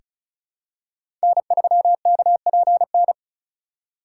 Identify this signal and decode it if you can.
Morse code, heard as a keyed tone, about 34 wpm, 700 Hz. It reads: N3KPN
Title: USA Call Signs - 5 Characters - 4x Character Spacing - 35wpm